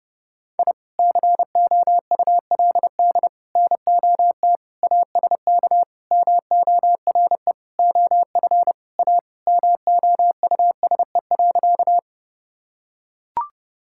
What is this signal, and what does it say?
Morse code, heard as a keyed tone, about 30 wpm, 700 Hz. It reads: I COULD NOT ASK MORE OF A MOUSE.
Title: Sentences from Top 1000 Words - Review Part 5 - 30wpm